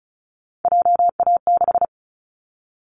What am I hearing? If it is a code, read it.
Morse code, heard as a keyed tone, about 34 wpm, 700 Hz. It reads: JA6
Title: Callsign Prefixes for Top 25 DXCC - 35wpm